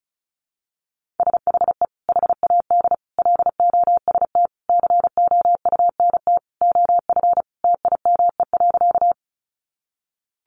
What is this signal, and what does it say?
Morse code, heard as a keyed tone, about 34 wpm, 700 Hz. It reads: SHE HAD LOST COUNT OF TIME.
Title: Sentences from Top 1500 Words 35wpm